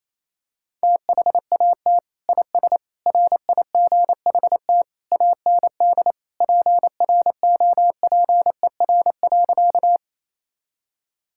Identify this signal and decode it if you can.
Morse code, heard as a keyed tone, about 28 wpm, 700 Hz. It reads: THAT IS RIGHT AND PROPER.
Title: Sentences from Top 1000 Words - Review Part 5 - 28wpm